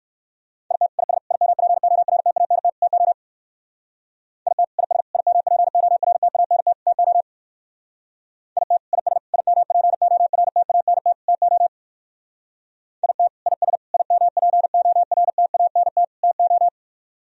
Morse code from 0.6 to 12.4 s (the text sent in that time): IT IS IMPORTANT TO IT IS IMPORTANT TO IT IS IMPORTANT TO